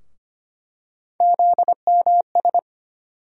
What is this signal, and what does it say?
Morse code, heard as a keyed tone, about 25 wpm, 700 Hz. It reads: ZMS
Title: USA Call Sign Postfixes - 3 Letters 25wpm